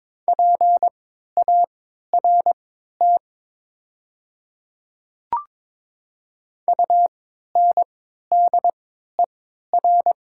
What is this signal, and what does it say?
Morse code, heard as a keyed tone, about 22 wpm, spaced as for 12 wpm, 700 Hz. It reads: PART UNDER
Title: 3x Character Spacing - Top 300 Words - 22wpm